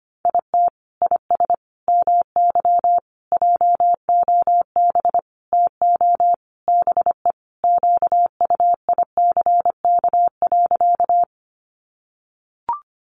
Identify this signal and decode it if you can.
Morse code, heard as a keyed tone, about 25 wpm, 700 Hz. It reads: IT IS MY JOB TO BE QUICK.
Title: Sentences from Top 600 Words 25wpm